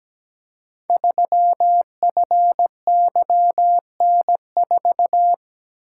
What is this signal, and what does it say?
Morse code, heard as a keyed tone, about 17 wpm, 700 Hz. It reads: 3FYN4
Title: Code Groups - 5 Letter:Numbers - 17wpm